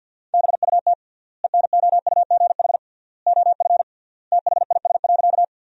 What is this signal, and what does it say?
Morse code, heard as a keyed tone, about 52 wpm, 700 Hz. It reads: BUT ENOUGH OF THIS.
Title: 3x Word-Spacing - Sentences from Top 400 Words - 50wpm